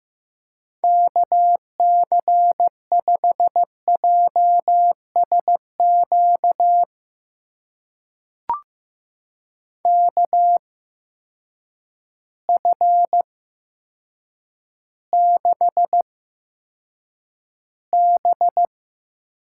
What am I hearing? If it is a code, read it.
Morse code, heard as a keyed tone, about 15 wpm, spaced as for 8 wpm, 700 Hz. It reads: KC5JSQ KF6B
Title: USA Call Signs - 6 Characters - 8x Character Spacing - 15wpm